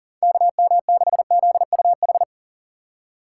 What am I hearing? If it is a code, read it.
Morse code, heard as a keyed tone, about 40 wpm, 700 Hz. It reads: KM6ZUH
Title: USA Call Signs - 6 Characters - 6x Character Spacing - 40wpm